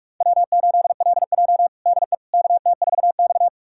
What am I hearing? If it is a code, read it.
Morse code, heard as a keyed tone, about 41 wpm, 700 Hz. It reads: W8LJ DE KT4X
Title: USA Callsign DE Callsign - 4 Characters - 45wpm